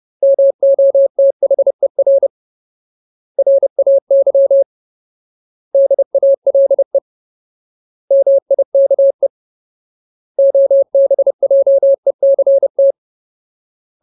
Morse code, heard as a keyed tone, about 30 wpm, 550 Hz. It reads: MOTHER RAY DALE MIKE OBJECT
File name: Ultimate Word List / 30WPM Ultimate